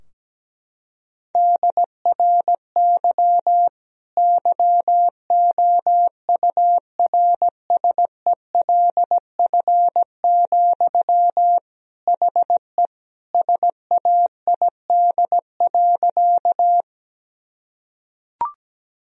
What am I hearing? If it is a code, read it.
Morse code, heard as a keyed tone, about 17 wpm, 700 Hz. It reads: DRY YOURSELF, HE SAID.
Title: Sentences from Top 1000 Words - Review Part 3 - 17wpm